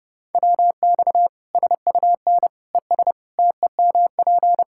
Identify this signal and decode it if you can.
Morse code, heard as a keyed tone, about 30 wpm, 700 Hz. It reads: WX SUN ES TEMP
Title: Practice QSO - Exchange 2 - Info - 30wpm